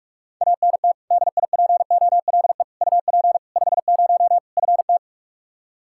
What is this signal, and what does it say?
Morse code, heard as a keyed tone, about 46 wpm, 700 Hz. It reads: ANT DIPOLE UP 50 FT
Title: K1USN SST QSO Practice 45wpm